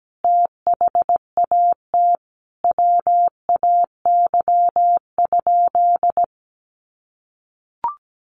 Morse code, heard as a keyed tone, about 17 wpm, 700 Hz. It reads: THAT WAY?
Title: Sentences from Top 1100 Words 17wpm